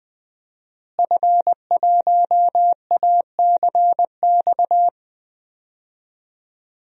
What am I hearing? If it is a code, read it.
Morse code, heard as a keyed tone, about 20 wpm, 700 Hz. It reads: F1ACX
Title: DXCC-100 Call Signs - 5 Characters - Repeated 3x - 20wpm